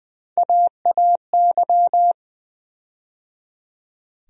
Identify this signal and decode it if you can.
Morse code, heard as a keyed tone, about 20 wpm, 700 Hz. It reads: AAY
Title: USA Call Sign Postfixes - 3 Letters 20wpm